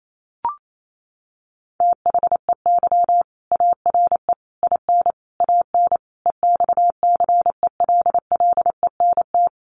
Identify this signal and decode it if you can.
Morse code, heard as a keyed tone, about 28 wpm, 700 Hz. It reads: THEY ARE IN AN EXCELLENT
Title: Sentences from Top 1000 Words 28wpm